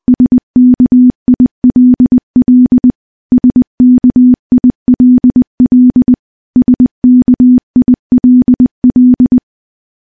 Morse code, heard as a keyed tone, about 20 wpm, 250 Hz. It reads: SKILL SKILL SKILL
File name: morse-training-20WPM